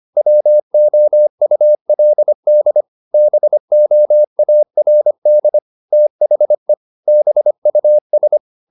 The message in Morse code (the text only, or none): WOULD BOARD THE BUS